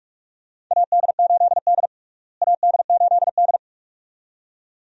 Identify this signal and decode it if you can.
Morse code, heard as a keyed tone, about 42 wpm, 700 Hz. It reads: AD8D AD8D
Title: USA Call Signs - 4 Characters - Repeated 2x - 45wpm